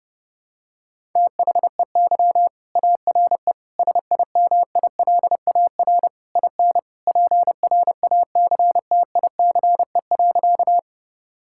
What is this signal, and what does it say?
Morse code, heard as a keyed tone, about 30 wpm, 700 Hz. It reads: THEY ARE SIMILAR IN PRACTICE.
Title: Sentences from Top 1000 Words - Review Part 5 - 30wpm